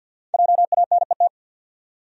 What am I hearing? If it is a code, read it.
Morse code, heard as a keyed tone, about 51 wpm, 700 Hz. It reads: JANET
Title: USA Common Names - Max 5 Letters - 50wpm